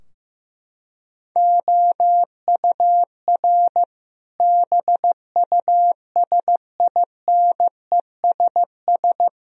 Morse code, heard as a keyed tone, about 15 wpm, 700 Hz. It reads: OUR BUSINESS
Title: Sentences from Top 1000 Words - Review Part 4 - 15wpm